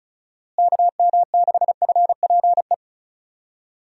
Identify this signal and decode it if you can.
Morse code, heard as a keyed tone, about 36 wpm, 700 Hz. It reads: KM6FPE
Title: USA Call Signs - 6 Characters - Repeated 2x - 35wpm